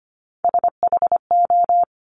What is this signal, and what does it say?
Morse code, heard as a keyed tone, about 25 wpm, 700 Hz. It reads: SHO
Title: Top Three Letter Combinations in Words 25wpm